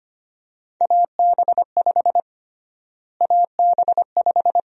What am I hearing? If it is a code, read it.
Morse code, heard as a keyed tone, about 25 wpm, 700 Hz. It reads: AB5 AB5
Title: USA Call Sign Prefixes - Repeated 2x - 25wpm